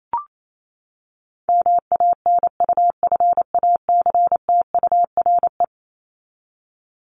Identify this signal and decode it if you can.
Morse code, heard as a keyed tone, about 28 wpm, 700 Hz. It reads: MANUFACTURE
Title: ING Suffix - Top 100 Words with Base Words 28wpm